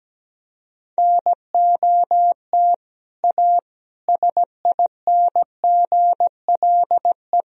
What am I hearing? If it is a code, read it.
Morse code, heard as a keyed tone, about 17 wpm, 700 Hz. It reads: NOT A SINGLE
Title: Sentences from Top 1000 Words - Review Part 4 - 17wpm